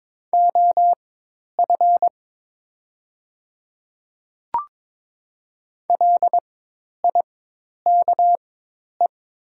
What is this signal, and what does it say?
Morse code, heard as a keyed tone, about 22 wpm, spaced as for 10 wpm, 700 Hz. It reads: OF LIKE
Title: 4x Character Spacing - Top 100 Words - 22wpm